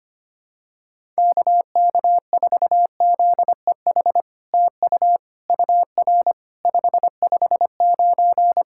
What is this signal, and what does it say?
Morse code, heard as a keyed tone, about 25 wpm, 700 Hz. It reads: KK4ZEH TU UR 559
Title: SOTA QSO Practice 25wpm